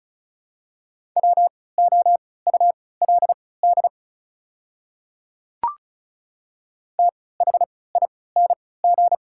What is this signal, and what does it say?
Morse code, heard as a keyed tone, about 36 wpm, spaced as for 20 wpm, 700 Hz. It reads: WOULD THING
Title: 3x Character Spacing - Top 100 Words - 35wpm